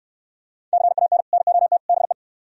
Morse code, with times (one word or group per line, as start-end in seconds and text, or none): 0.7-1.2 CAN
1.3-1.8 NOT
1.9-2.1 BE